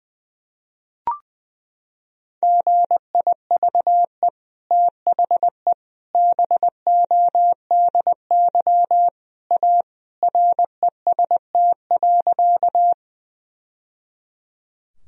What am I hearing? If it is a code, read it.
Morse code, heard as a keyed tone, about 20 wpm, 700 Hz. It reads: GIVE THE BODY A REST.
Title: Sentences from Top 1000 Words - Review Part 2 - 20wpm